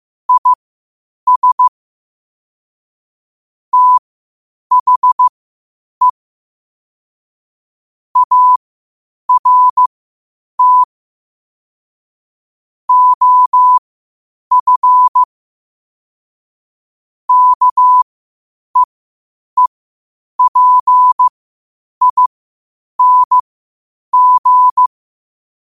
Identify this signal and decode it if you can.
Morse code, heard as a keyed tone, about 15 wpm, spaced as for 8 wpm, 1000 Hz. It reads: IS THE ART OF KEEPING
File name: qotdfiles / 8wpm